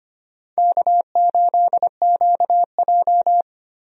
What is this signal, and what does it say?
Morse code, heard as a keyed tone, about 25 wpm, 700 Hz. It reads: K8QJ
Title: USA Call Signs - 4 Characters - Repeated 3x - 25wpm